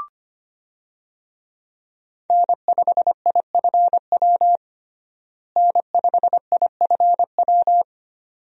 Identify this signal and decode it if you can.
Morse code, heard as a keyed tone, about 25 wpm, 700 Hz. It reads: N5IFW N5IFW
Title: USA Call Signs - 5 Characters - Repeated 2x - 25wpm